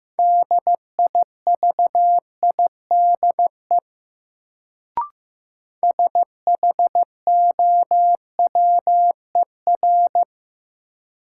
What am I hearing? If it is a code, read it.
Morse code, heard as a keyed tone, about 15 wpm, 700 Hz. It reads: DIVIDE SHOWER
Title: Sentences from Top 1600 Words - New Word Drill Alt - 15wpm